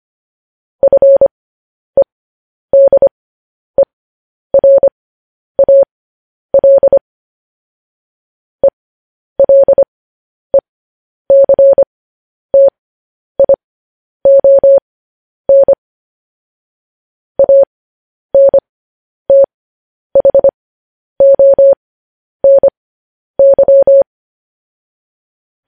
Morse code, heard as a keyed tone, about 25 wpm, spaced as for 10 wpm, 550 Hz. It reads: FEDERAL ELECTION ANTHONY
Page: News Headlines in Morse Code at 10 WPM